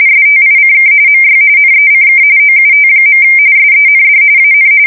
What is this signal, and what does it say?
RTTY, about 45.45 baud. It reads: STICK MY NECK OUT. SEND AN
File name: fsk170 / USS-New-Jersey